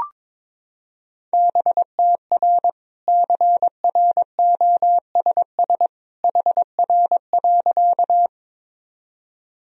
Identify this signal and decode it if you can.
Morse code, heard as a keyed tone, about 22 wpm, 700 Hz. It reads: BTR CROSS HR.
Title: Sentences from Top 400 Words and Abbreviations 22wpm